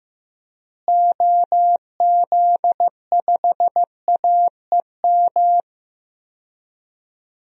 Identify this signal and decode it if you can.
Morse code, heard as a keyed tone, about 15 wpm, 700 Hz. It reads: OZ5AEM
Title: DXCC-100 Call Signs - 6 Characters - Repeated 3x - 15wpm